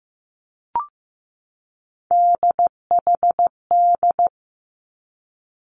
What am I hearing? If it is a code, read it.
Morse code, heard as a keyed tone, about 15 wpm, 700 Hz. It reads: DHD
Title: Three Challenging Letter-Number Combinations 15wpm